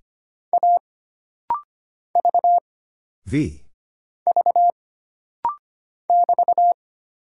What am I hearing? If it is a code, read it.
Morse code, heard as a keyed tone, about 25 wpm, 700 Hz. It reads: A V V <BT>
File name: Single Character (Letter:Number:Punctuation:Prosign) - Rapid-Fire - 25wpm